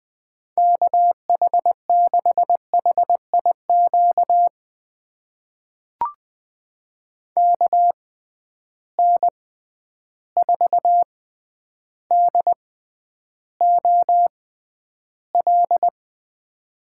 Morse code, heard as a keyed tone, about 20 wpm, spaced as for 10 wpm, 700 Hz. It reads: KH6HIQ KN4DOL
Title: USA Call Signs - 6 Characters - 6x Character Spacing - 20wpm